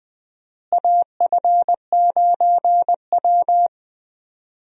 Morse code, heard as a keyed tone, about 20 wpm, 700 Hz. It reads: AF9W
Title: USA Call Signs - 4 Characters - Repeated 3x - 20wpm